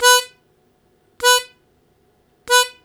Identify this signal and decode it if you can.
There is no code.